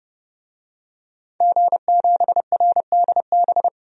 Morse code, heard as a keyed tone, about 31 wpm, 700 Hz. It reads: G7RDB
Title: DXCC-100 Call Signs - 5 Characters - 30wpm